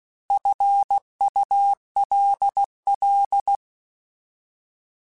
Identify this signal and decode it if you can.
Morse code, heard as a keyed tone, about 16 wpm, 800 Hz. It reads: FULL
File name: Alphabetical / Alphabetical mp3s / Fs @15